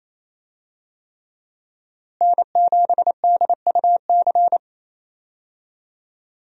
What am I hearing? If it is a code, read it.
Morse code, heard as a keyed tone, about 28 wpm, 700 Hz. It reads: N7DUC
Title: USA Call Signs - 5 Characters - Repeated 3x - 28wpm